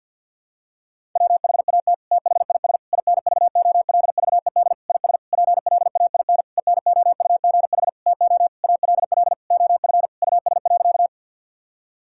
Morse code, heard as a keyed tone, about 49 wpm, 700 Hz. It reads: WHAT THIS INVOLVED IS PLAIN ENOUGH TO ALL OF US.
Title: Sentences from Top 1000 Words - Review Part 5 - 50wpm